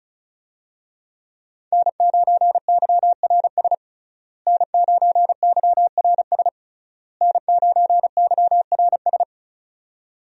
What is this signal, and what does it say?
Morse code, heard as a keyed tone, about 36 wpm, 700 Hz. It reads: N9YRS N9YRS N9YRS
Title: USA Call Signs - 5 Characters - Repeated 3x - 35wpm